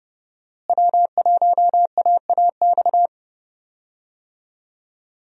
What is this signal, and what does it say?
Morse code, heard as a keyed tone, about 30 wpm, 700 Hz. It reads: W1AAX
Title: USA Call Signs - 5 Characters - Repeated 2x - 30wpm